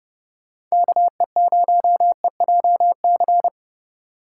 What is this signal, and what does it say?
Morse code, heard as a keyed tone, about 35 wpm, 700 Hz. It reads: KE0EJC
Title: Speed Racing - USA Call Signs 30wpm